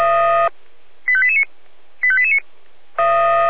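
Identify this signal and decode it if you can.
Five-tone selective call (ZVEI), heard as quick runs of five tones, tones of about 70 ms. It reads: 85889 85889
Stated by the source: ZVEI five-tone selective call received on an SDR.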